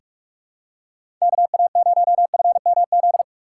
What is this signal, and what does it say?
Morse code, heard as a keyed tone, about 46 wpm, 700 Hz. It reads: KA0FMZ
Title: USA Call Signs - 6 Characters - 6x Character Spacing - 45wpm